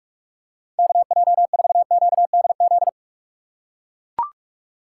Morse code, heard as a keyed tone, about 44 wpm, 700 Hz. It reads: KJ4QDZ
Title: USA Call Signs - 6 Characters - 3x Character Spacing - 45wpm